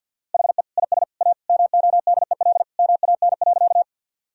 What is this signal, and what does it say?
Morse code, heard as a keyed tone, about 50 wpm, 700 Hz. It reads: HE IS A MODEL MAN.
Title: Sentences from Top 1000 Words - Review Part 5 - 50wpm